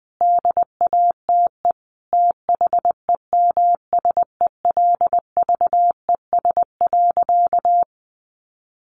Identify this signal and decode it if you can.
Morse code, heard as a keyed tone, about 20 wpm, 700 Hz. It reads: DATE THEMSELVES.
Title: Sentences from Top 1300 Words 20wpm